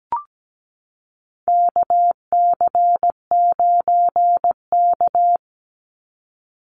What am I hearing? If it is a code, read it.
Morse code, heard as a keyed tone, about 17 wpm, 700 Hz. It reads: KC9K
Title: Call Signs Repeated 2x - 17wpm